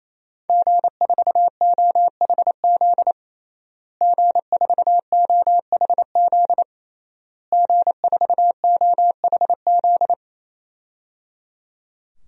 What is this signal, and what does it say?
Morse code, heard as a keyed tone, about 28 wpm, 700 Hz. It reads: G4OHZ G4OHZ G4OHZ
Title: DXCC-100 Call Signs - 5 Characters - Repeated 3x - 28wpm